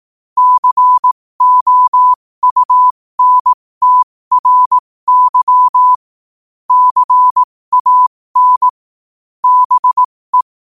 Morse code, heard as a keyed tone, about 18 wpm, spaced as for 15 wpm, 1000 Hz. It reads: COUNTRY CAN BE
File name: qotdfiles / 15wpm